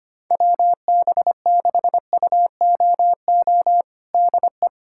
Morse code, heard as a keyed tone, about 25 wpm, 700 Hz. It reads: WB6UOO DE
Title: USA Callsign DE Callsign 25wpm